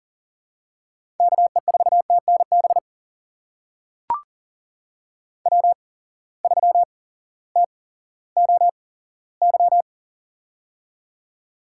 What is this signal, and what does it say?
Morse code, heard as a keyed tone, about 40 wpm, spaced as for 18 wpm, 700 Hz. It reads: KE4TNB W3TOY
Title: USA Call Signs - 8x Character Spacing - 40wpm